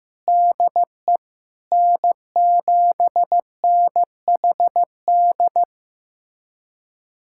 Morse code, heard as a keyed tone, about 15 wpm, 700 Hz. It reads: DE N7NHD
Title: USA Callsign DE Callsign - 5 Characters - 15wpm